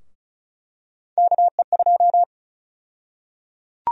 Morse code, heard as a keyed tone, about 34 wpm, 700 Hz. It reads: KE2
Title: USA Call Sign Prefixes - 35wpm